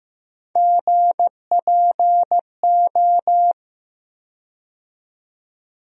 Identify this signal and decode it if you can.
Morse code, heard as a keyed tone, about 15 wpm, 700 Hz. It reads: GPO
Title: USA Call Sign Postfixes - 3 Letters 15wpm